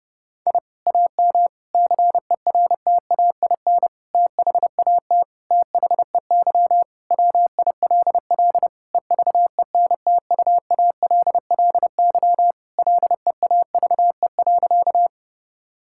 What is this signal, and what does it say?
Morse code, heard as a keyed tone, about 30 wpm, 700 Hz. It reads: I AM CERTAIN THAT THEY WILL EVENTUALLY LEAVE.
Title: Sentences from Top 1000 Words - Review Part 6 - 30wpm